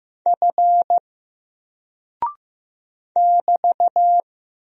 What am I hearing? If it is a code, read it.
Morse code, heard as a keyed tone, about 15 wpm, 700 Hz. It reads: F <BT>
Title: Single Character (Letter:Number:Punctuation:Prosign) 15wpm